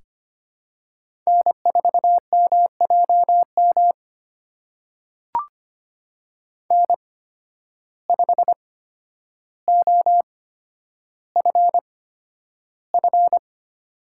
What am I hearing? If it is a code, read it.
Morse code, heard as a keyed tone, about 25 wpm, spaced as for 11 wpm, 700 Hz. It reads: N4MJM N5OFF
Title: USA Call Signs - 8x Character Spacing - 25wpm